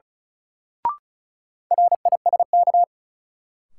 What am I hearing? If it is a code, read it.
Morse code, heard as a keyed tone, about 36 wpm, 700 Hz. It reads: RISK
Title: Sentences from Top 600 Words - New Word Drill - 35wpm